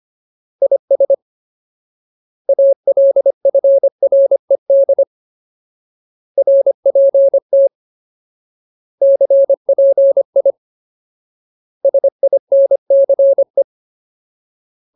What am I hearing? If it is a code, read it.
Morse code, heard as a keyed tone, about 25 wpm, 550 Hz. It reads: IS ALFRED RPT CPI SINCE
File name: Ultimate Word List / 25WPM Ultimate